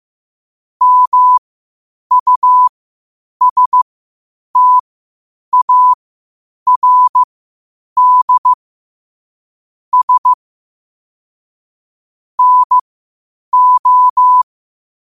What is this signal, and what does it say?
Morse code, heard as a keyed tone, about 15 wpm, spaced as for 8 wpm, 1000 Hz. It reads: MUSTARD S NO
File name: qotdfiles / 8wpm